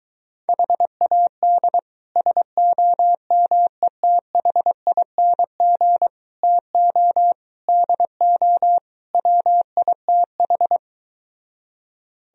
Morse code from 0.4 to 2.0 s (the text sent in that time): HAD